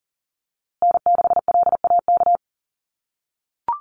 Morse code, heard as a keyed tone, about 40 wpm, 700 Hz. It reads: N6LAK
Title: USA Call Signs Repeated 2x 40wpm